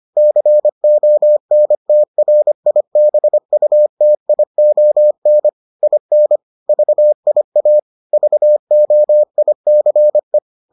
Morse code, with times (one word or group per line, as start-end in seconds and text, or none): 0.2-5.5 CONTRIBUTION
5.8-6.4 IN
6.7-7.8 VIA
8.1-10.4 VOICE